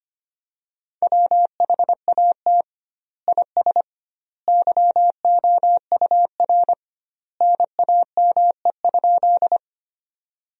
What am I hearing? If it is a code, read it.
Morse code, heard as a keyed tone, about 25 wpm, 700 Hz. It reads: WHAT IS YOUR NAME?
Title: 2x Word-Spacing - Sentences from Top 200 Words - 25wpm